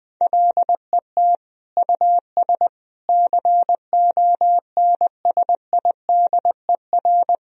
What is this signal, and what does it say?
Morse code, heard as a keyed tone, about 20 wpm, 700 Hz. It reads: LET US CONSIDER
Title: Sentences from Top 1000 Words - Review Part 4 - 20wpm